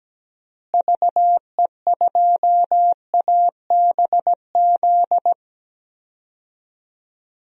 Morse code, heard as a keyed tone, about 17 wpm, 700 Hz. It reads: VE2ABZ
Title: DXCC-100 Call Signs - 6 Characters - Repeated 2x - 17wpm